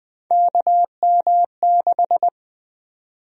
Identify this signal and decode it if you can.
Morse code, heard as a keyed tone, about 20 wpm, 700 Hz. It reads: KM6
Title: USA Call Sign Prefixes - 20wpm